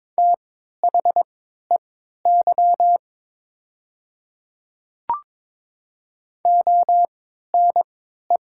Morse code, heard as a keyed tone, about 21 wpm, spaced as for 12 wpm, 690 Hz. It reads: THEY ONE
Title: 3x Character Spacing - Top 100 Words - 22wpm